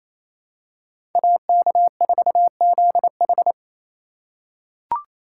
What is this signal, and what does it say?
Morse code, heard as a keyed tone, about 27 wpm, 700 Hz. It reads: AK4ZH
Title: USA Call Signs - 3x Character Spacing - 28wpm